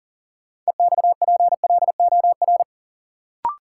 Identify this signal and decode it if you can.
Morse code, heard as a keyed tone, about 40 wpm, 700 Hz. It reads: EXPLOR
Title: Sentences from Top 1100 Words - New Word Drill Alt - 40wpm